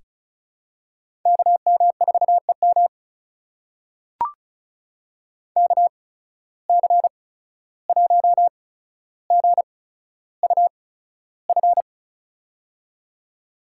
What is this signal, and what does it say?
Morse code, heard as a keyed tone, about 36 wpm, spaced as for 14 wpm, 700 Hz. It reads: KM4EM KC1GUF
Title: USA Call Signs - 8x Character Spacing - 35wpm